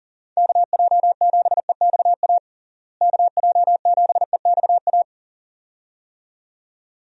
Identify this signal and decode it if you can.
Morse code, heard as a keyed tone, about 40 wpm, 700 Hz. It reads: KJ7EXA KJ7EXA
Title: USA Call Signs - 6 Characters - Repeated 2x - 40wpm